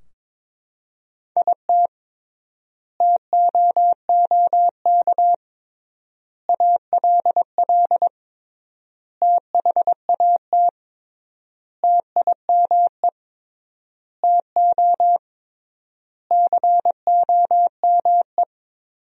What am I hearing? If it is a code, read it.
Morse code, heard as a keyed tone, about 22 wpm, 700 Hz. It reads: IT TOOK ALL THAT TIME TO COME
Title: 3x Word-Spacing - Sentences from Top 300 Words - 22wpm